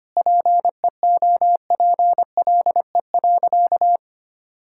Morse code, heard as a keyed tone, about 25 wpm, 700 Hz. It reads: PEOPLE.
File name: practice / 25wpm / Sentences with 9-Word Phrases - 25wpm